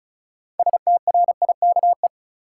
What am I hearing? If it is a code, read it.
Morse code, heard as a keyed tone, about 36 wpm, 700 Hz. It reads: STRIKE
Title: Six Letter Words - 35wpm